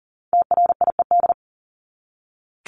Morse code, heard as a keyed tone, about 38 wpm, 700 Hz. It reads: TRIED
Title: ED Suffix - Top 100 Words with Base Words 40wpm